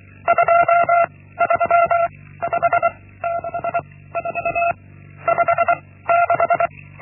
Morse code, heard as a keyed tone, about 24 wpm, 680 Hz. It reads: 2 3 5 6 4 5 6
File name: Linksmakalnis-1